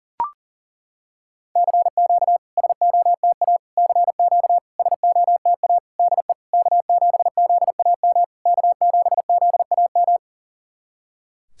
Morse code, heard as a keyed tone, about 40 wpm, 690 Hz. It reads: CQ SOTA CQ SOTA DE K7ZAM K7ZAM
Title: SOTA QSO Practice 40wpm